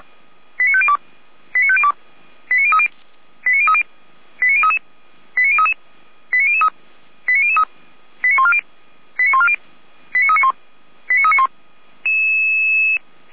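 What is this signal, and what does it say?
Five-tone selective call (ZVEI), heard as quick runs of five tones, tones of about 70 ms. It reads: 89502 89502 89030 89030 89033 89033 89003 89003 89140 89140 89391 89391
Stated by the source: ZVEI five-tone selective call received on SDR.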